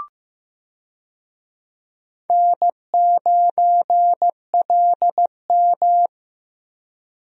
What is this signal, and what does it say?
Morse code, heard as a keyed tone, about 15 wpm, 700 Hz. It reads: N9LM
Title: USA Call Signs - 4 Characters - Repeated 3x - 15wpm